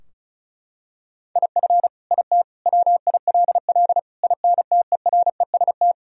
Morse code, heard as a keyed tone, about 36 wpm, 700 Hz. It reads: IF IT WILL INTEREST
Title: Sentences from Top 1000 Words - Review Part 2 - 35wpm